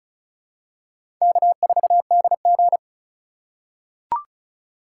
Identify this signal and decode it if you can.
Morse code, heard as a keyed tone, about 36 wpm, 700 Hz. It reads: K4DG
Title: USA Call Signs - 4x Character Spacing - 35wpm